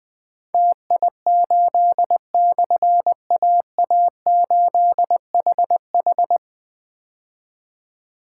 Morse code, heard as a keyed tone, about 20 wpm, 700 Hz. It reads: TI8/AA8HH
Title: Difficult Call Signs 20wpm